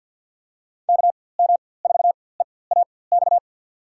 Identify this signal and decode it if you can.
Morse code, heard as a keyed tone, about 48 wpm, spaced as for 23 wpm, 700 Hz. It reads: KM4EAX
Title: USA Call Signs - 6 Characters - 4x Character Spacing - 50wpm